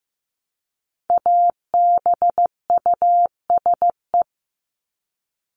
Morse code, heard as a keyed tone, about 15 wpm, 700 Hz. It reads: ABUSE
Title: Top 1251-1500 Words 15wpm